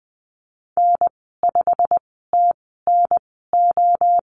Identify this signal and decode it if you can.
Morse code, heard as a keyed tone, about 20 wpm, spaced as for 15 wpm, 700 Hz. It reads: N5TNO
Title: 173 Lesson 28g - Number 9 - Call Signs 20wpm